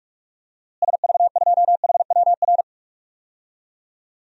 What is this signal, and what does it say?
Morse code, heard as a keyed tone, about 42 wpm, 700 Hz. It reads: SV2HWR
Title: DXCC-100 Call Signs - 6 Characters - Repeated 2x - 45wpm